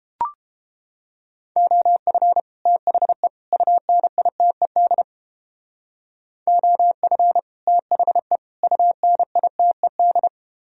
Morse code, heard as a keyed tone, about 30 wpm, 700 Hz. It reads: OF THE UNITED OF THE UNITED